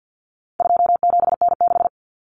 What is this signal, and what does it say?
Morse code, heard as a keyed tone, about 49 wpm, 700 Hz. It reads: 27N6
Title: Common Cut Numbers - 4-Digit 50wpm